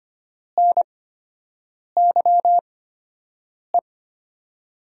Morse code, heard as a keyed tone, about 25 wpm, spaced as for 7 wpm, 700 Hz. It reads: NYE
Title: USA Call Signs - 6 Characters - 8x Character Spacing - 25wpm